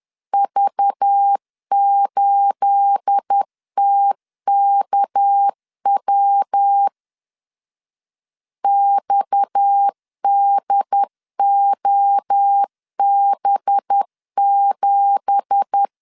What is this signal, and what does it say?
Morse code, heard as a keyed tone, about 10 wpm, 800 Hz. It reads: V8TKW XDOB7
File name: LEZIONI / 9 2867 10wpm 2733